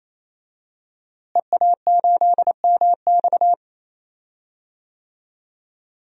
Morse code, heard as a keyed tone, about 28 wpm, 700 Hz. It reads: EA8MX
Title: DXCC-100 Call Signs - 5 Characters - Repeated 3x - 28wpm